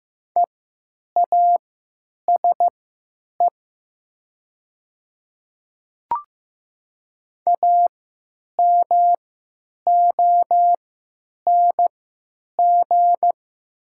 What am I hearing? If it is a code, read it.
Morse code, heard as a keyed tone, about 15 wpm, spaced as for 9 wpm, 700 Hz. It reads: EASE AMONG
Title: 3x Character Spacing - Top 500 Words - 15wpm